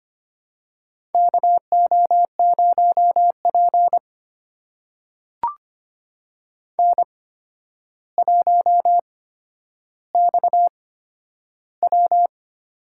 Morse code, heard as a keyed tone, about 25 wpm, spaced as for 11 wpm, 700 Hz. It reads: KO0P N1XW
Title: USA Call Signs - 4 Characters - 8x Character Spacing - 25wpm